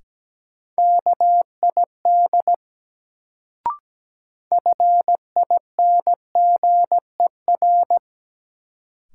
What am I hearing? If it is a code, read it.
Morse code, heard as a keyed tone, about 17 wpm, 700 Hz. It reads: KID FINGER
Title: Sentences from Top 1600 Words - New Word Drill - 17wpm